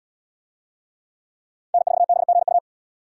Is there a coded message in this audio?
Yes: Morse code from 1.7 to 2.6 s, heard as a keyed tone, about 74 wpm, 700 Hz. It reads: N4BGU